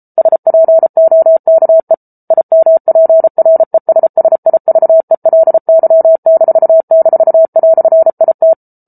Morse code, heard as a keyed tone, about 34 wpm, 650 Hz. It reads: SPOKE IMPRESSIVELY--"IT